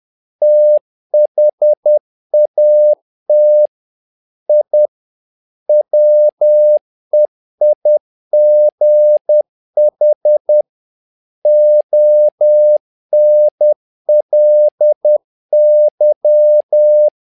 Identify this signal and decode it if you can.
Morse code, heard as a keyed tone, about 10 wpm, 600 Hz. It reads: THAT I WEIGH ONLY
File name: learning center / tower trouble / tower trouble 10wpm